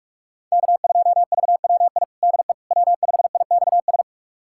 Morse code, heard as a keyed tone, about 46 wpm, 700 Hz. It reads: K2VWI DE W5IXS
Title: USA Callsign DE Callsign - 5 Characters - 45wpm